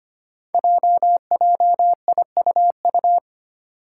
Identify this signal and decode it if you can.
Morse code, heard as a keyed tone, about 25 wpm, 700 Hz. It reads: JJIUU
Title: Five Challenging Letter-Number Combinations 25wpm